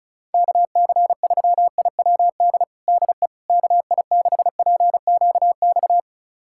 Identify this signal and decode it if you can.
Morse code, heard as a keyed tone, about 36 wpm, 690 Hz. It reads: KC3IWD DE KI6PQX
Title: USA Callsign DE Callsign - 6 Characters - 35wpm